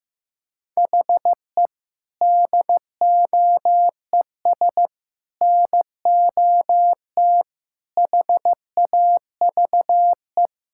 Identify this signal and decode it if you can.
Morse code, heard as a keyed tone, about 15 wpm, 700 Hz. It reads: HE DOES NOT HAVE
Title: Sentences from Top 1000 Words - Review Part 5 - 15wpm